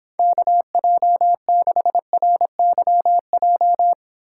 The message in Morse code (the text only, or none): KJ6RYJ